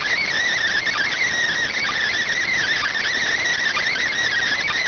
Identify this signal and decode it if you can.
Emergency Alert System (SAME) tones, heard as harsh buzzing bursts. It is heard from 0.9 to 4.1 s.